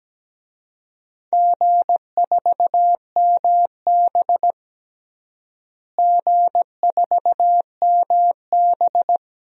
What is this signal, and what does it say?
Morse code, heard as a keyed tone, about 17 wpm, 700 Hz. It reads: G4MB G4MB
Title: DXCC-100 Call Signs - 4 Characters - Repeated 2x - 17wpm